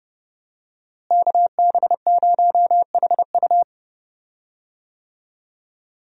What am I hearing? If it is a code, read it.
Morse code, heard as a keyed tone, about 30 wpm, 700 Hz. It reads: KB0HU
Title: USA Call Signs - 5 Characters - Repeated 2x - 30wpm